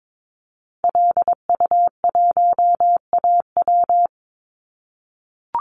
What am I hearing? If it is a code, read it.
Morse code, heard as a keyed tone, about 22 wpm, 700 Hz. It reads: LU1AW
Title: Call Signs - 22wpm